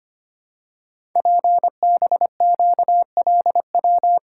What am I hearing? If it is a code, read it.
Morse code, heard as a keyed tone, about 25 wpm, 700 Hz. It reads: PBQLW
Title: Code Groups - 5 Letters - 25wpm